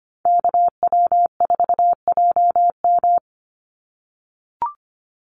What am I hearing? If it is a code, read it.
Morse code, heard as a keyed tone, about 25 wpm, 700 Hz. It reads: KW4JM
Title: NA SOTA Activator Callsigns 25wpm